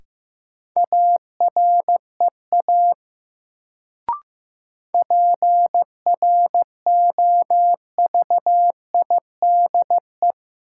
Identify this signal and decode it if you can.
Morse code, heard as a keyed tone, about 15 wpm, 700 Hz. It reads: AREA PROVIDE
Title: Sentences from Top 1000 Words - Review Part 2 - New Word Drill - 15wpm